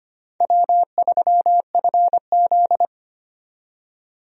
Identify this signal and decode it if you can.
Morse code, heard as a keyed tone, about 25 wpm, 700 Hz. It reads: W3FZ
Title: USA Call Signs - 4 Characters - 25wpm